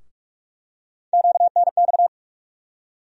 Morse code, heard as a keyed tone, about 44 wpm, 700 Hz. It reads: QNX
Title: USA Call Sign Postfixes - 3 Letters 45wpm